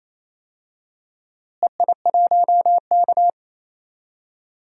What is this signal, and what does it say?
Morse code, heard as a keyed tone, about 28 wpm, 700 Hz. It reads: EI1K
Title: DXCC-100 Call Signs - 4 Characters - Repeated 3x - 28wpm